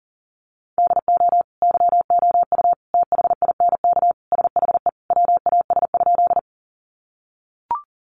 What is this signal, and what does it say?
Morse code, heard as a keyed tone, about 40 wpm, 700 Hz. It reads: DO YOU THINK SHE WAS?
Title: Sentences from Top 200 Words 40wpm